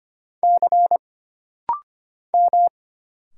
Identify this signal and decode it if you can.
Morse code, heard as a keyed tone, about 25 wpm, 700 Hz. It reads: C M
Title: Single Letters - Rapid-Fire 25wpm